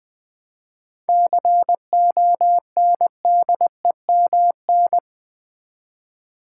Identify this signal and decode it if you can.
Morse code, heard as a keyed tone, about 20 wpm, 700 Hz. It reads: CONDEMN
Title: 7 Letter Words - First 3 Letters Given - 20wpm